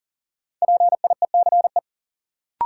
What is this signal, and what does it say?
Morse code, heard as a keyed tone, about 40 wpm, 700 Hz. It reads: PIECE
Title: Sets of 3 Words - New Word Drill - 40wpm